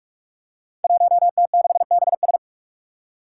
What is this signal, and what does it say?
Morse code, heard as a keyed tone, about 44 wpm, 700 Hz. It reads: 1T6BS
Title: Five Challenging Letter-Number Combinations 45wpm